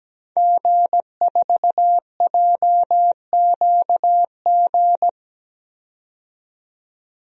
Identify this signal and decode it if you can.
Morse code, heard as a keyed tone, about 17 wpm, 700 Hz. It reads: G4JQG
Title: DXCC-100 Call Signs - 5 Characters - 17wpm